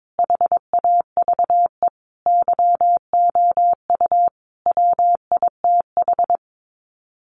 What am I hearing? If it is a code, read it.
Morse code, heard as a keyed tone, about 22 wpm, 700 Hz. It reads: HAVE YOU WITH
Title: Sets of 5 Words Encore2 22wpm